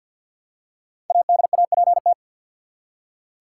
Morse code, heard as a keyed tone, about 47 wpm, 700 Hz. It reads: ADAPT
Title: Five Letter Words - 50wpm